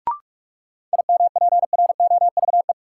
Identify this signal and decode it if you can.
Morse code, heard as a keyed tone, about 43 wpm, 700 Hz. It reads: IMPROVE
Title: Sentences from Top 800 Words - New Word Drill - 45wpm